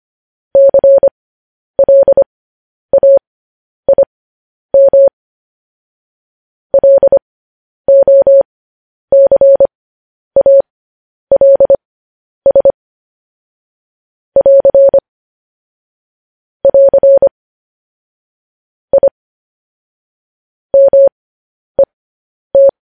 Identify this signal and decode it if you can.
Morse code, heard as a keyed tone, about 25 wpm, spaced as for 10 wpm, 550 Hz. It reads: CLAIM LOCALS <AR> <AR> I MET
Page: news-headlines-in-morse-code-at-10-wpm